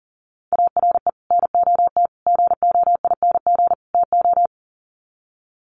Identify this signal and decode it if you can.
Morse code, heard as a keyed tone, about 40 wpm, 700 Hz. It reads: ARE NOT GOING TO
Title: Sets of 4 Words 40wpm